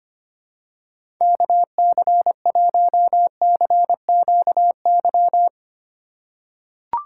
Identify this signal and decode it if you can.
Morse code, heard as a keyed tone, about 25 wpm, 700 Hz. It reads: KC1CQY
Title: USA Call Signs - 6 Characters - 6x Character Spacing - 25wpm